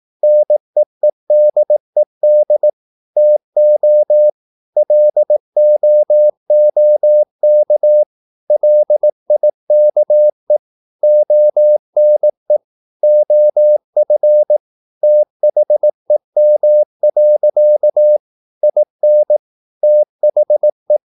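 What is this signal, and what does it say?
Morse code, heard as a keyed tone, about 18 wpm, 600 Hz. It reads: NEEDED TO LOOK LIKE ONE OF THEM. IN THE
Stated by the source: Morse practice at 18 wpm, English text, from CW med Gnister.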